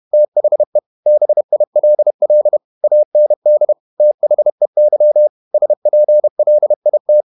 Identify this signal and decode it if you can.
Morse code, heard as a keyed tone, about 31 wpm, 600 Hz. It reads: THE BILL AND THEY SPLIT
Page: Never 31wpm | CW med Gnister